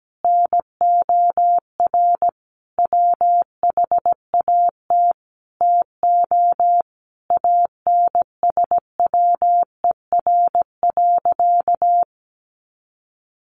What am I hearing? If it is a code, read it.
Morse code, heard as a keyed tone, about 17 wpm, 700 Hz. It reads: NOR WHAT TO ANSWER.
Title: Sentences from Top 1500 Words 17wpm